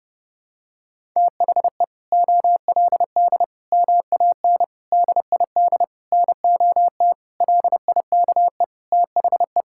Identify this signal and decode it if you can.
Morse code, heard as a keyed tone, about 32 wpm, 700 Hz. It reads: THE OLD MAN DID NOT LIKE THE
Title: Sentences from Top 1000 Words - Review Part 6 - 30wpm